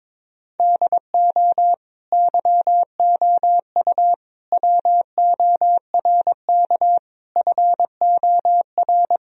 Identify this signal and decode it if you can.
Morse code, heard as a keyed tone, about 21 wpm, 700 Hz. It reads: DO YOU WORK FOR
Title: Sentences from Top 1000 Words - Review Part 3 - 22wpm